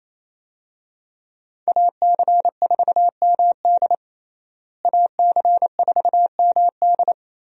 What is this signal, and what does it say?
Morse code, heard as a keyed tone, about 28 wpm, 700 Hz. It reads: AC4MD AC4MD
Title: USA Call Signs - 5 Characters - Repeated 3x - 28wpm